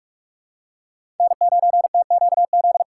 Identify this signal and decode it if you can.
Morse code, heard as a keyed tone, about 47 wpm, 700 Hz. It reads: N9TQZ
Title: USA Call Signs - 5 Characters - 3x Character Spacing - 45wpm